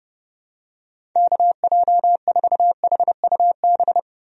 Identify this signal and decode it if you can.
Morse code, heard as a keyed tone, about 30 wpm, 700 Hz. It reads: KJ4HUB
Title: USA Call Signs - 6 Characters - 30wpm